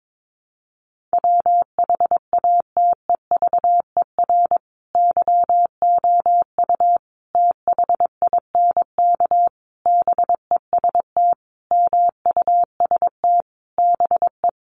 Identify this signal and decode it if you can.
Morse code, heard as a keyed tone, about 22 wpm, 700 Hz. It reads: WHATEVER YOU THINK BEST MUST BE
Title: Sentences from Top 1100 Words 22wpm